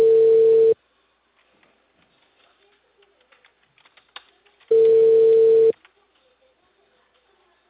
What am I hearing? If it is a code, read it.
Ringback tone (China, 450 Hz).